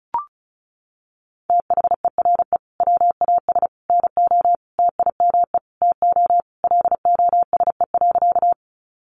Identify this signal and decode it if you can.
Morse code, heard as a keyed tone, about 34 wpm, 700 Hz. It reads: THERE WAS NO TIME TO LOSE.
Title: Sentences from Top 1100 Words 35wpm